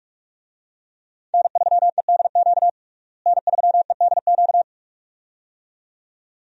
Morse code, heard as a keyed tone, about 42 wpm, 700 Hz. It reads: N3EDQ N3EDQ
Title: USA Call Signs - 5 Characters - Repeated 2x - 45wpm